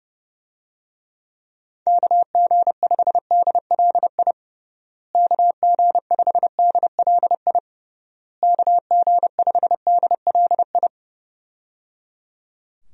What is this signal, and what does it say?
Morse code, heard as a keyed tone, about 29 wpm, 700 Hz. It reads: KG5DLI KG5DLI KG5DLI
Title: USA Call Signs - 6 Characters - Repeated 3x - 30wpm